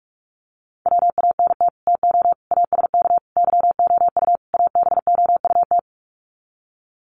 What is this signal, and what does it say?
Morse code, heard as a keyed tone, about 44 wpm, 700 Hz. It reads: WANT TO ASK YOU ABOUT
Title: Sets of 5 Words Encore2 45wpm